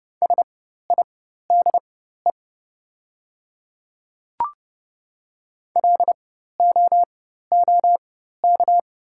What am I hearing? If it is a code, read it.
Morse code, heard as a keyed tone, about 30 wpm, spaced as for 14 wpm, 700 Hz. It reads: SIDE LOOK
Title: 4x Character Spacing - Top 100 Words - 30wpm